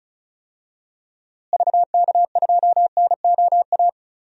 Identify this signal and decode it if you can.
Morse code, heard as a keyed tone, about 36 wpm, 700 Hz. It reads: VK2NOA
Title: DXCC-100 Call Signs - 6 Characters - Repeated 2x - 35wpm